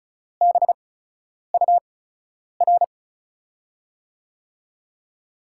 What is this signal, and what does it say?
Morse code, heard as a keyed tone, about 36 wpm, spaced as for 10 wpm, 700 Hz. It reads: BUR
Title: USA Call Signs - 5 Characters - 8x Character Spacing - 35wpm